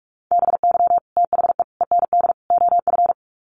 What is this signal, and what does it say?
Morse code, heard as a keyed tone, about 44 wpm, 700 Hz. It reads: BY THE END OF
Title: Sets of 4 Words 45wpm